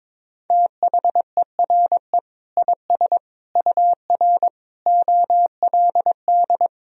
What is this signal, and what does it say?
Morse code, heard as a keyed tone, about 22 wpm, 700 Hz. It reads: THERE IS UR OLD
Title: Sentences from Top 500 Words and Abbreviations 22wpm